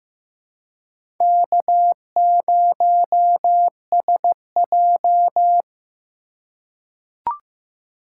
Morse code, heard as a keyed tone, about 15 wpm, 700 Hz. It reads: K0SJ
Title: USA Call Signs - 8x Character Spacing - 15wpm